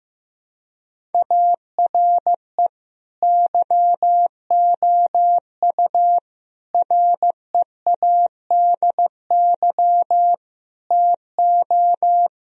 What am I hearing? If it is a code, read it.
Morse code, heard as a keyed tone, about 16 wpm, 700 Hz. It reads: ARE YOU READY TO
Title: Sentences from Top 1000 Words - Review Part 1 - 15wpm